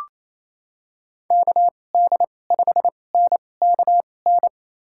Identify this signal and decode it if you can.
Morse code, heard as a keyed tone, about 28 wpm, spaced as for 20 wpm, 700 Hz. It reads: KD5NKN
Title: USA Call Signs - 2x Character Spacing - 28wpm